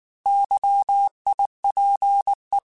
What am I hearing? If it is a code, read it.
Morse code, heard as a keyed tone, about 19 wpm, 800 Hz. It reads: YIPE
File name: Alphabetical / Alphabetical mp3s / Ys @18